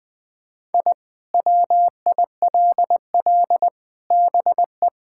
Morse code, heard as a keyed tone, about 20 wpm, 700 Hz. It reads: I WILL BE
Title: Sentences with 4-Word Phrases - Spoken Beginning - 20wpm